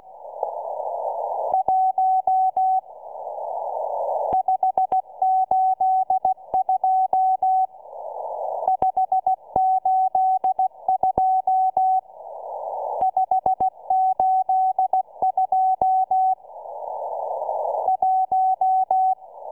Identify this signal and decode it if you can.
Morse code, heard as a keyed tone, about 16 wpm, 750 Hz. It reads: 1 582 582 582 1